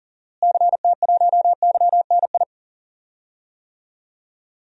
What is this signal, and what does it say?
Morse code, heard as a keyed tone, about 39 wpm, 700 Hz. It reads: CT1YNI
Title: DXCC-100 Call Signs - 6 Characters - Repeated 2x - 40wpm